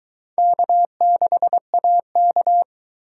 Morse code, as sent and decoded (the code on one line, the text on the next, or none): -.- -.... .- -.-
K6AK